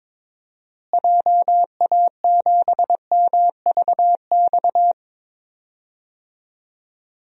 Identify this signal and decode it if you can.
Morse code, heard as a keyed tone, about 22 wpm, 700 Hz. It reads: JA7MVX
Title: DXCC-100 Call Signs - 6 Characters - Repeated 3x - 22wpm